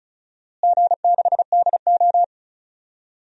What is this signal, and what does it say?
Morse code, heard as a keyed tone, about 34 wpm, 700 Hz. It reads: G6DO
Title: DXCC-100 Call Signs - 4 Characters - Repeated 3x - 35wpm